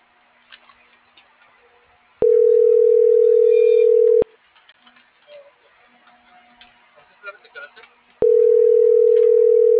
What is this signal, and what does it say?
Ringback tone (North America), 440 + 480 Hz.